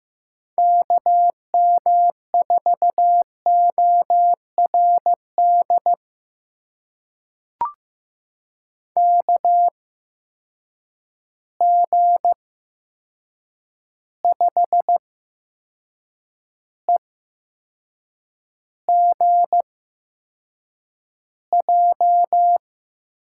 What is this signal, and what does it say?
Morse code, heard as a keyed tone, about 15 wpm, spaced as for 6 wpm, 700 Hz. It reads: KM4ORD KG5EGJ
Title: USA Call Signs - 8x Character Spacing - 15wpm